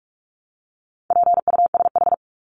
Morse code, heard as a keyed tone, about 44 wpm, 700 Hz. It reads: PUSH
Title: Top 1001-1250 Words 45wpm